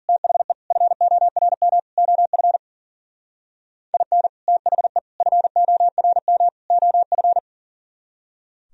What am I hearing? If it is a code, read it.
Morse code, heard as a keyed tone, about 49 wpm, 700 Hz. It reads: THE FORM OF IN THE FORM OF
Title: Speed-Racing - Sets of 4 Words 40wpm